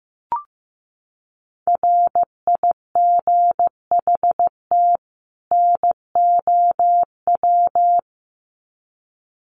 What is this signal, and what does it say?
Morse code, heard as a keyed tone, about 15 wpm, 700 Hz. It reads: RIGHT NOW
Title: Sets of 2 Words 15wpm